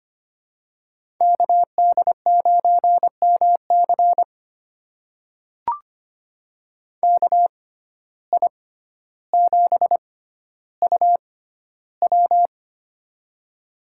Morse code, heard as a keyed tone, about 25 wpm, spaced as for 13 wpm, 700 Hz. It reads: KD9MC KI7UW
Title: USA Call Signs - 5 Characters - 6x Character Spacing - 25wpm